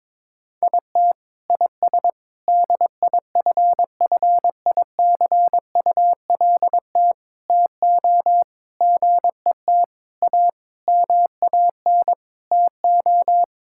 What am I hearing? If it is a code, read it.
Morse code, heard as a keyed tone, about 22 wpm, 700 Hz. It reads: IT IS DIFFICULT TO GET A MAN TO
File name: Sentences with 9-Word Phrases - 22wpm